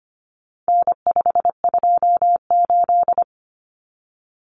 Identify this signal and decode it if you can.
Morse code, heard as a keyed tone, about 25 wpm, 700 Hz. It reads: N528
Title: Common Cut Numbers - 4-Digit 25wpm